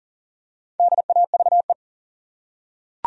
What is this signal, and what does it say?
Morse code, heard as a keyed tone, about 40 wpm, 700 Hz. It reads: DAVE
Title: USA Common Names - Max 4 Letters - 40wpm